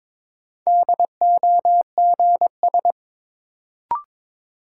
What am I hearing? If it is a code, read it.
Morse code, heard as a keyed tone, about 22 wpm, 700 Hz. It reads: DOGS
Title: Sentences from Top 1600 Words - New Word Drill Alt - 22wpm